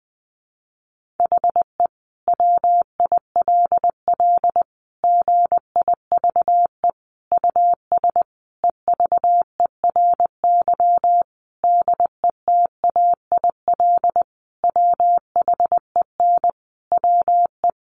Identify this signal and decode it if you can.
Morse code, heard as a keyed tone, about 20 wpm, 700 Hz. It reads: HE WILL GIVE US EVERY DETAIL WHEN WE
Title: Sentences from Top 1500 Words 20wpm